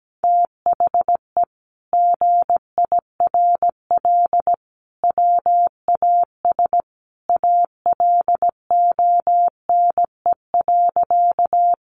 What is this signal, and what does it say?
Morse code, heard as a keyed tone, about 17 wpm, 700 Hz. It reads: THE GIRL WAS ALONE.
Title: Sentences from Top 1000 Words 17wpm